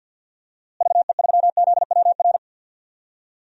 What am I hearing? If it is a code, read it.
Morse code, heard as a keyed tone, about 51 wpm, 700 Hz. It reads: VE3ZWR
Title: DXCC-100 Call Signs - 6 Characters - Repeated 3x - 50wpm